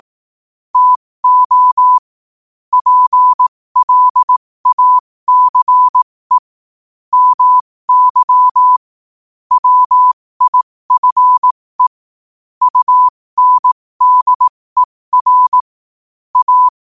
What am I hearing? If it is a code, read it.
Morse code, heard as a keyed tone, about 18 wpm, spaced as for 15 wpm, 1000 Hz. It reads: TO PLACE MY WIFE UNDER A
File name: qotdfiles / 15wpm